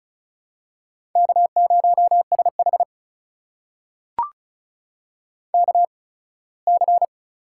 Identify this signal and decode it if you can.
Morse code, heard as a keyed tone, about 36 wpm, spaced as for 21 wpm, 700 Hz. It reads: K0SH KC
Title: USA Call Signs - 8x Character Spacing - 35wpm